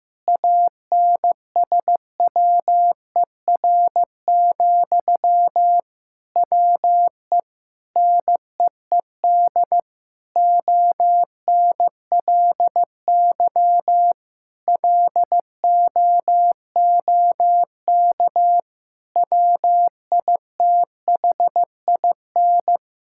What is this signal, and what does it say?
Morse code, heard as a keyed tone, about 15 wpm, 700 Hz. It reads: ANSWER, WE NEED ONLY LOOK WITHIN
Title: Sentences from Top 1600 Words 15wpm